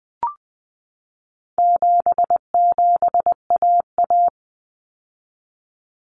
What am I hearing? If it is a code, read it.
Morse code, heard as a keyed tone, about 20 wpm, 700 Hz. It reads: 77AA
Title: Common Cut Numbers - 4-Digit 20wpm